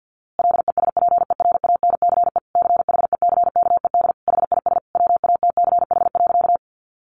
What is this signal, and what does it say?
Morse code, heard as a keyed tone, about 48 wpm, 700 Hz. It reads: LESPERANCE CHECKED HIS WATCH.
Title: A Sound Of Thunder 50wpm